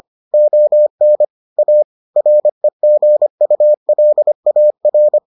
Morse code, heard as a keyed tone, about 25 wpm, 600 Hz. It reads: ON A REGULAR